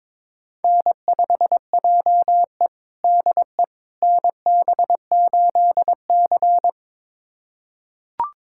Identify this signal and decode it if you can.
Morse code, heard as a keyed tone, about 22 wpm, 700 Hz. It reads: N5JE DE NB8C
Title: USA Callsign DE Callsign - 4 Characters - 22wpm